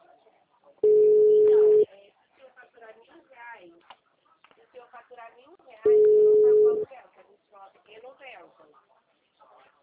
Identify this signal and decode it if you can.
Ringback tone (Europe, 425 Hz), with speech.